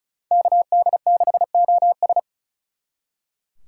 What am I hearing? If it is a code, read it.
Morse code, heard as a keyed tone, about 45 wpm, 700 Hz. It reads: KD6OS